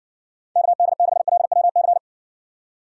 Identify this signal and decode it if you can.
Morse code, heard as a keyed tone, about 56 wpm, 700 Hz. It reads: KD6LWC